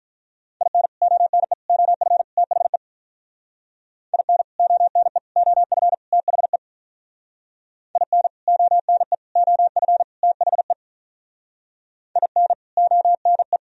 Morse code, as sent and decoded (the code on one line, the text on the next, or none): .. -. / --- -. / --- ..-. / - .... / .. -. / --- -. / --- ..-. / - .... / .. -. / --- -. / --- ..-. / - .... / .. -. / --- -. .
IN ONE OF THE IN ONE OF THE IN ONE OF THE IN ONE